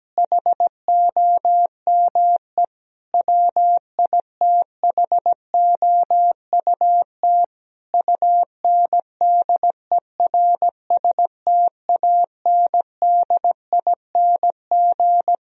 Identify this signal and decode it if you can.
Morse code, heard as a keyed tone, about 17 wpm, 700 Hz. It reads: HOME WITHOUT UNDERSTANDING
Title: Sentences from Top 1000 Words - Review Part 3 - 17wpm